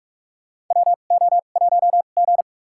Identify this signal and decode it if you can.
Morse code, heard as a keyed tone, about 43 wpm, spaced as for 32 wpm, 700 Hz. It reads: WO1G
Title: USA Call Signs - 2x Character Spacing - 45wpm